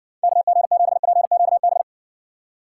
Morse code, heard as a keyed tone, about 60 wpm, 700 Hz. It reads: KG7PQB